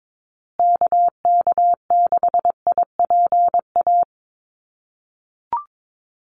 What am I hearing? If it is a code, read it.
Morse code, heard as a keyed tone, about 22 wpm, 700 Hz. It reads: KK6IPA
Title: USA Call Signs Repeated 2x 22wpm